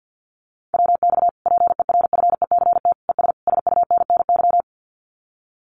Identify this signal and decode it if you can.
Morse code, heard as a keyed tone, about 48 wpm, 700 Hz. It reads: WX PERFECT ES SUNNY
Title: Common QSO Phrases Encore 50wpm